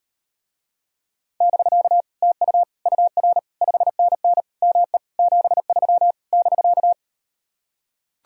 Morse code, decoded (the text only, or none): <BK> TU UR 5NN ME 73 <BK>